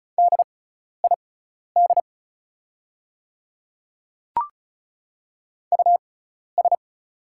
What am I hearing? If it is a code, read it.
Morse code, heard as a keyed tone, about 36 wpm, spaced as for 12 wpm, 690 Hz. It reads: DID US
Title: 6x Character Spacing - Top 100 Words - 35wpm